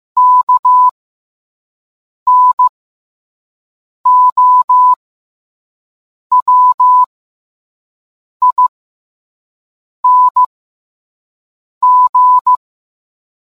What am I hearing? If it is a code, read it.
Morse code, heard as a keyed tone, about 15 wpm, spaced as for 5 wpm, 1000 Hz. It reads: KNOWING